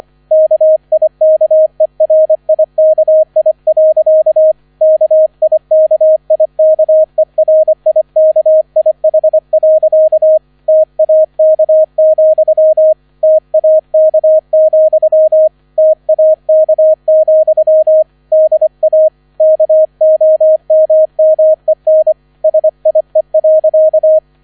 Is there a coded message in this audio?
Yes: Morse code from 0.3 to 24.2 s, heard as a keyed tone, about 25 wpm, 630 Hz, with radio noise. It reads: KIKERIKI. KIKIKERIKIH. TAK, TAK, TAK, DA KOMMEN SIE.